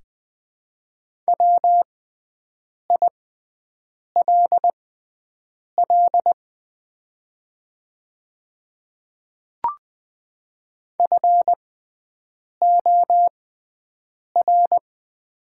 Morse code, heard as a keyed tone, about 20 wpm, spaced as for 7 wpm, 700 Hz. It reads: WILL FOR
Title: 6x Character Spacing - Top 100 Words - 20wpm